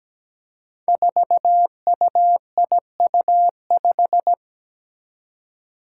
Morse code, heard as a keyed tone, about 17 wpm, 700 Hz. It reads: 4UIU5
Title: Five Challenging Letter-Number Combinations 17wpm